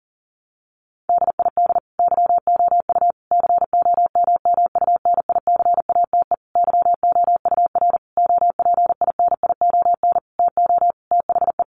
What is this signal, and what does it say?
Morse code, heard as a keyed tone, about 40 wpm, 700 Hz. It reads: DID YOU COMMUNICATE YOUR OPINION TO THE
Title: Sentences from Top 1300 Words 40wpm